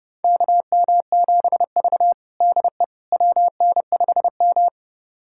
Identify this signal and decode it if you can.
Morse code, heard as a keyed tone, about 30 wpm, 700 Hz. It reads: KM7V DE WN5M
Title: USA Callsign DE Callsign - 4 Characters - 30wpm